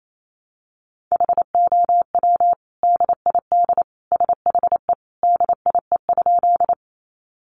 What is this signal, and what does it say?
Morse code, heard as a keyed tone, about 28 wpm, 700 Hz. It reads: HOW DID SHE DIE?
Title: Sentences from Top 1300 Words 28wpm